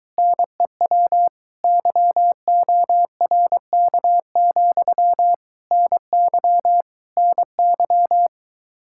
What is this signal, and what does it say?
Morse code, heard as a keyed tone, about 23 wpm, 700 Hz. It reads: NEW YORK, NY NY